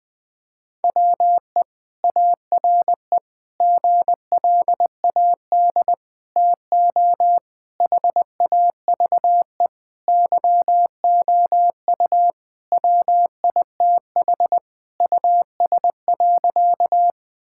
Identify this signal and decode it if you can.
Morse code, heard as a keyed tone, about 21 wpm, 700 Hz. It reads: WE ARE GLAD TO HAVE YOU WITH US.
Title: Sentences with 5-Word Phrases - 20wpm